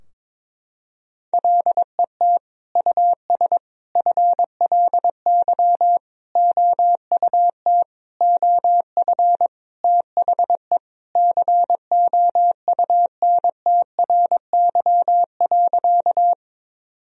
Morse code, heard as a keyed tone, about 22 wpm, 700 Hz. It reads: LET US FLY OUT OF THE COUNTRY.
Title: Sentences from Top 1000 Words - Review Part 2 - 22wpm